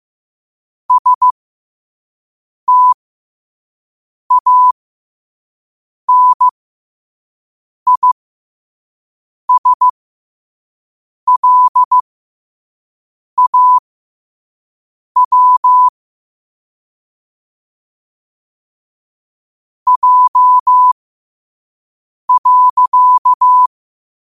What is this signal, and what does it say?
Morse code, heard as a keyed tone, about 15 wpm, spaced as for 5 wpm, 1000 Hz. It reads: STANISLAW J.
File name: qotdfiles / 5wpm